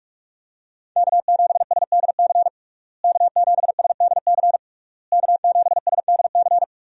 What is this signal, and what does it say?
Morse code, heard as a keyed tone, about 41 wpm, 700 Hz. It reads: K7SDC K7SDC K7SDC
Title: USA Call Signs - 5 Characters - Repeated 3x - 45wpm